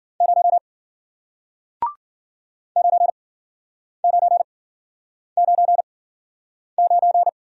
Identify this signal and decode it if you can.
Morse code, heard as a keyed tone, about 53 wpm, spaced as for 9 wpm, 700 Hz. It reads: 0 9999